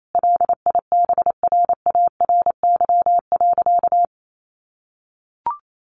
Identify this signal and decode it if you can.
Morse code, heard as a keyed tone, about 28 wpm, 700 Hz. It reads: LIBRARY.
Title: Sentences from Top 1300 Words 28wpm